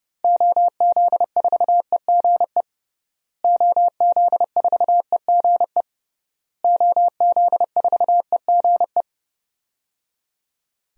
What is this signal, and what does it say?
Morse code, heard as a keyed tone, about 30 wpm, 700 Hz. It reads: OZ4EGE OZ4EGE OZ4EGE
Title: DXCC-100 Call Signs - 6 Characters - Repeated 3x - 30wpm